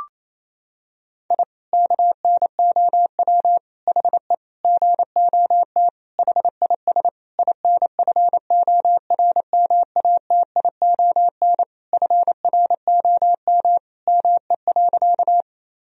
Morse code, heard as a keyed tone, about 27 wpm, 700 Hz. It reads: I KNOW HE GOT HIS INFORMATION FROM ME.
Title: Sentences from Top 1000 Words - Review Part 3 - 28wpm